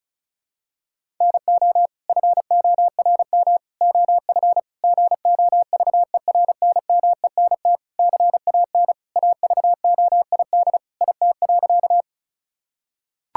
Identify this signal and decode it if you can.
Morse code, heard as a keyed tone, about 34 wpm, 700 Hz. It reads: NO FORM OF GOVERNMENT CAN AVOID IT.
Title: Sentences from Top 1000 Words - Review Part 6 - 35wpm